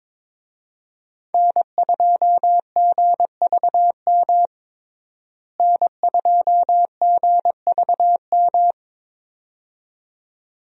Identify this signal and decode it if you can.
Morse code, heard as a keyed tone, about 22 wpm, 700 Hz. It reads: N2GVM N2GVM
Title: USA Call Signs - 5 Characters - Repeated 2x - 22wpm